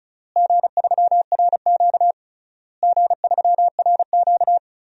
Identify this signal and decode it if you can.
Morse code, heard as a keyed tone, about 36 wpm, 700 Hz. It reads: G3RQ G3RQ
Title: DXCC-100 Call Signs - 4 Characters - Repeated 3x - 35wpm